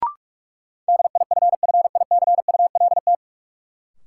Morse code, heard as a keyed tone, about 46 wpm, 700 Hz. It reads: DIFFICULT
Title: Sentences from Top 600 Words - New Word Drill - 45wpm